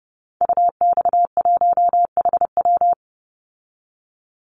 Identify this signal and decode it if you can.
Morse code, heard as a keyed tone, about 30 wpm, 700 Hz. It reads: UX1HW
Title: Call Signs Repeated 3x - 30wpm